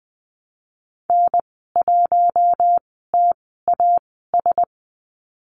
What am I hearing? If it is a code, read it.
Morse code, heard as a keyed tone, about 20 wpm, spaced as for 15 wpm, 700 Hz. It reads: N1TAS
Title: 048 Lesson 9g - Number 4 - Call Signs 20wpm